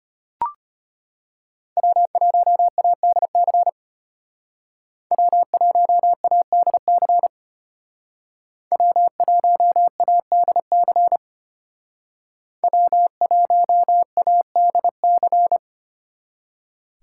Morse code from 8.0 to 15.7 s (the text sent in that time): W1ADC W1ADC